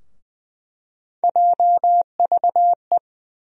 Morse code, heard as a keyed tone, about 20 wpm, 700 Hz. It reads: JVE
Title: USA Call Sign Postfixes - 3 Letters 20wpm